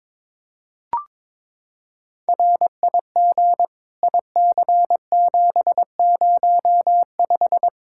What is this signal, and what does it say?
Morse code, heard as a keyed tone, about 22 wpm, 700 Hz. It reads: RIG IC705
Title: K1USN SST QSO Practice 22wpm